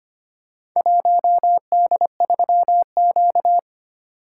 Morse code, heard as a keyed tone, about 25 wpm, 700 Hz. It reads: 1D3Q
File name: practice / 25wpm / Four Letter-Number Combinations - 25wpm